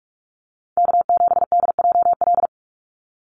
Morse code, heard as a keyed tone, about 44 wpm, 700 Hz. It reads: K7DJL
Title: NA SOTA Chaser Callsigns 45wpm